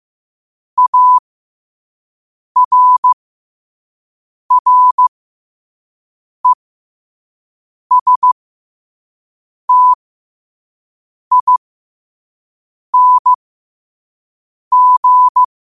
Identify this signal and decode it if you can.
Morse code, heard as a keyed tone, about 15 wpm, spaced as for 5 wpm, 1000 Hz. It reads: ARRESTING
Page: New quotes every day in morse code at 5 Words per minute.